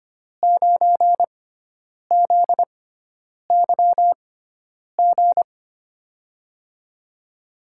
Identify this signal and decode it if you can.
Morse code, heard as a keyed tone, about 25 wpm, spaced as for 9 wpm, 700 Hz. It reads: 9ZYG
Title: USA Call Signs - 5 Characters - 6x Character Spacing - 25wpm